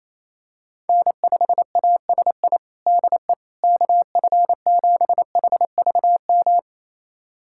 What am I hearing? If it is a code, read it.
Morse code, heard as a keyed tone, about 28 wpm, 700 Hz. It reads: N5ASI DE KF7HVM
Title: USA Callsign DE Callsign 28wpm